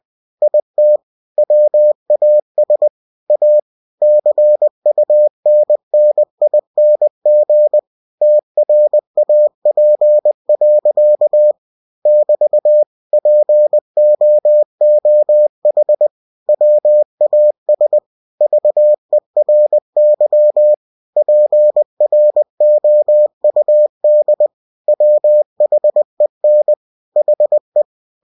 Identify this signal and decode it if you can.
Morse code, heard as a keyed tone, about 20 wpm, 600 Hz. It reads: IT WAS A CUNNING TRAP. <BT> POOH WAS VERY PROUD WHEN HE
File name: cw-books / winnie-the-pooh / winnie-20wpm-page-0035